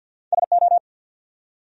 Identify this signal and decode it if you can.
Morse code, heard as a keyed tone, about 53 wpm, 700 Hz. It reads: SO
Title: Top 100 Words 50wpm